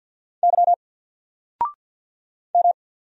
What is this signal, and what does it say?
Morse code, heard as a keyed tone, about 50 wpm, 700 Hz. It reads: Y M
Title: Single Character (Letter:Number:Punctuation:Prosign) 50wpm